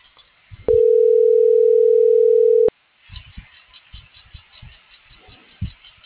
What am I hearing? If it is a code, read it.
Ringback tone (North America), 440 + 480 Hz.